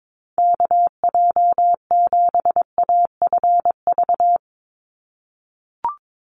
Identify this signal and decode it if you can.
Morse code, heard as a keyed tone, about 22 wpm, 700 Hz. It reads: KJ7AFV
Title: USA Call Signs 22wpm